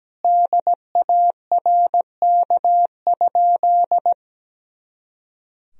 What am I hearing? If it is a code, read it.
Morse code, heard as a keyed tone, about 17 wpm, 700 Hz. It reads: DARK?
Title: Sentences from Top 500 Words and Abbreviations 17wpm